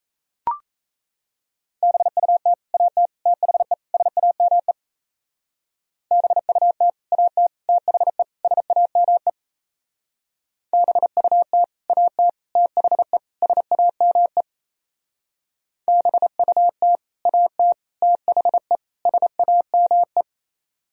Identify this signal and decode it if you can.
Morse code, heard as a keyed tone, about 33 wpm, 700 Hz. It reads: BUT AT THE SAME BUT AT THE SAME BUT AT THE SAME BUT AT THE SAME
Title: Speed-Racing - Sets of 4 Words 28wpm